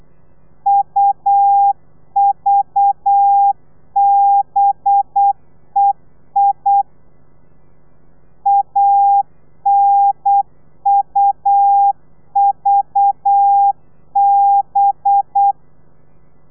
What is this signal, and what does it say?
Morse code, heard as a keyed tone, about 8 wpm, 800 Hz. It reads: UVBEI ANUVB
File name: telegrafia / i2tfj / Lez4 35